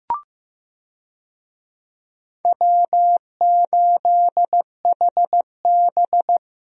Morse code, heard as a keyed tone, about 15 wpm, 700 Hz. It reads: W8HB
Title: USA Call Signs - 4 Characters - Repeated 3x - 15wpm